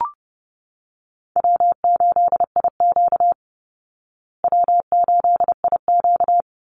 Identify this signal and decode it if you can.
Morse code, heard as a keyed tone, about 30 wpm, 700 Hz. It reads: W8IQ W8IQ
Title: Easy Call Signs Repeated 3x - 30wpm